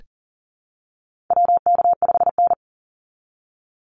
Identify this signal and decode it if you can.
Morse code, heard as a keyed tone, about 40 wpm, 700 Hz. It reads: WK5N
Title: Call Signs Repeated 3x - 40wpm